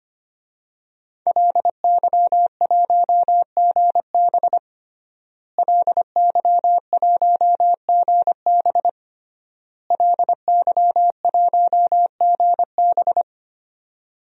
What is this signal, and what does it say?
Morse code, heard as a keyed tone, about 25 wpm, 700 Hz. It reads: LY1GB LY1GB LY1GB
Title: DXCC-100 Call Signs - 5 Characters - Repeated 3x - 25wpm